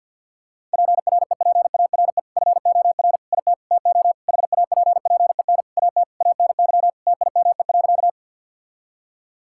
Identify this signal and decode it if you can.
Morse code, heard as a keyed tone, about 52 wpm, 700 Hz. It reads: PREPARE FOR IT TO HAPPEN AT ANY TIME.
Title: Sentences from Top 1000 Words - Review Part 6 - 50wpm